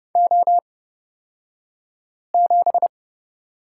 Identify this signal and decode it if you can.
Morse code, heard as a keyed tone, about 30 wpm, spaced as for 5 wpm, 700 Hz. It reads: O7
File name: Lesson 37d - Letter x - Two Characters 30wpm